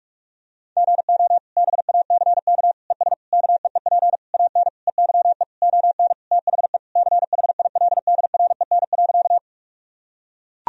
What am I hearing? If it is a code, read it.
Morse code, heard as a keyed tone, about 41 wpm, 700 Hz. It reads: GO BACK ES KEEP AN EYE ON THE CHILDREN.
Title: Sentences from Top 300 Words and Abbreviations 45wpm